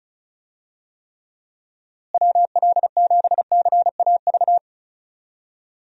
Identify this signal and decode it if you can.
Morse code, heard as a keyed tone, about 34 wpm, 700 Hz. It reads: WL7CAV
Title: USA Call Signs - 6 Characters - Repeated 3x - 35wpm